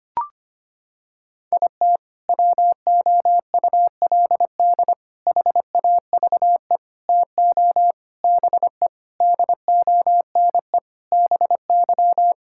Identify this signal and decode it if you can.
Morse code, heard as a keyed tone, about 25 wpm, 700 Hz. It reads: IT WOULD HAVE TO BE DONE BY
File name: practice / 25wpm / Sentences with 4-Word Phrases - 25wpm